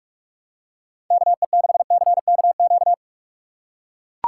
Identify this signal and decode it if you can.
Morse code, heard as a keyed tone, about 44 wpm, 700 Hz. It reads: KE6CKQ
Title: USA Call Signs - 6 Characters - 3x Character Spacing - 45wpm